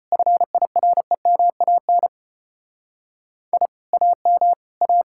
Morse code, heard as a keyed tone, about 32 wpm, 700 Hz. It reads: FIREMAN I AM A